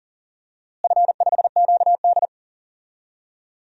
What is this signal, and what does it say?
Morse code, heard as a keyed tone, about 40 wpm, 700 Hz. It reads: F5QD
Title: DXCC-100 Call Signs - 4 Characters - Repeated 2x - 40wpm